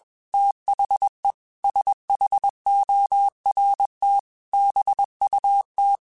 Morse code, heard as a keyed tone, about 21 wpm, 800 Hz. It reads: THE SHORT BUT